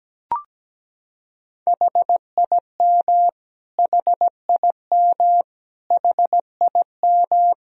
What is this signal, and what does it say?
Morse code, heard as a keyed tone, about 17 wpm, 700 Hz. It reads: HIM HIM HIM
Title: Top 100 Words Repeated 17wpm